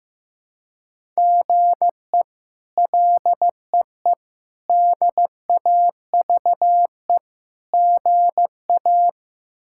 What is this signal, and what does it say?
Morse code, heard as a keyed tone, about 15 wpm, 700 Hz. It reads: GE LEE DAVE GA
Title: K1USN SST QSO Practice - 15wpm